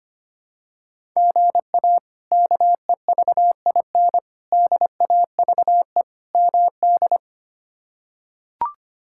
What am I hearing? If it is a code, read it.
Morse code, heard as a keyed tone, about 25 wpm, 700 Hz. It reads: GA KEVIN DAVE MD
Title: K1USN SST QSO Practice - 25wpm